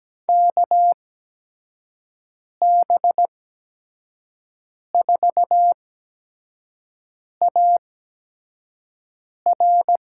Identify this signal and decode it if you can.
Morse code, heard as a keyed tone, about 17 wpm, spaced as for 5 wpm, 700 Hz. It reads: KB4AR
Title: USA Call Signs - 6 Characters - 8x Character Spacing - 17wpm